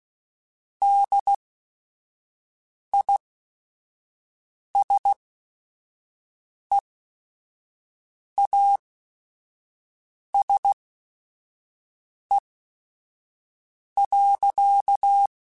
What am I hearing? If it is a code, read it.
Morse code, heard as a keyed tone, about 16 wpm, spaced as for 5 wpm, 800 Hz. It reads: DISEASE.
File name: Quotes / Quotes-mp3s / HoofInMouth @5